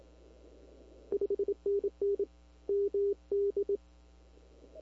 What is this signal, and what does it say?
Morse code, heard as a keyed tone, about 26 wpm, 400 Hz. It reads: 5NN MD
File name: CW / 2005-k5tr arrl cw dx -21:00